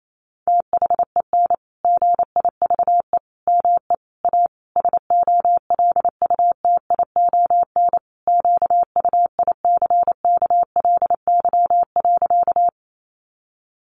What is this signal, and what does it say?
Morse code, heard as a keyed tone, about 28 wpm, 700 Hz. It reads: THEN GIVE ME A SOLUTION QUICKLY.
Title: Sentences from Top 1300 Words 28wpm